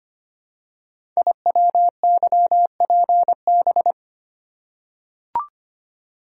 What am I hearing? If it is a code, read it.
Morse code, heard as a keyed tone, about 25 wpm, 700 Hz. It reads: IWYPB
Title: Code Groups - 5 Letters - 25wpm